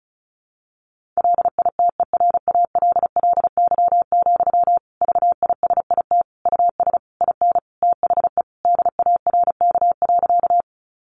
Morse code, heard as a keyed tone, about 34 wpm, 700 Hz. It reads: LITERALLY, VISIT US IN THE DARK.
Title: Sentences from Top 1400 Words 35wpm